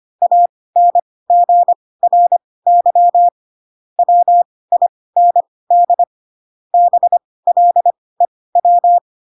Morse code, heard as a keyed tone, about 25 wpm, spaced as for 18 wpm, 700 Hz. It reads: ANGRY WIND BLEW
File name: Pinocchio / Pinocchio ch6 12 25 ik1hge